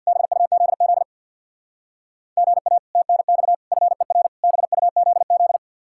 Morse code, heard as a keyed tone, about 60 wpm, 700 Hz. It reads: BUZZ GA TNX FER BUZZ
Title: Speed-Racing Common QSO Phrases - 50wpm